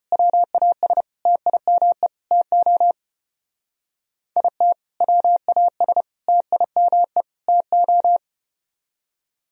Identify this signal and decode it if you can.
Morse code, heard as a keyed tone, about 34 wpm, 700 Hz. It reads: WAS TIME TO IT WAS TIME TO